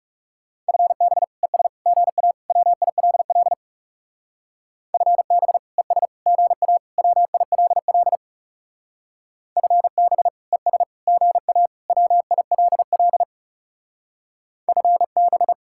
Morse code, heard as a keyed tone, about 36 wpm, 700 Hz. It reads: FB ES GA WILL FB ES GA WILL FB ES GA WILL FB
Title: Speed-Racing Common QSO Phrases - 30wpm